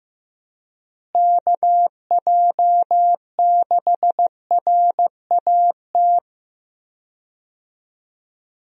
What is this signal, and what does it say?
Morse code, heard as a keyed tone, about 15 wpm, 700 Hz. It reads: KJ6RAT
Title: USA Call Signs - 6 Characters - Repeated 3x - 15wpm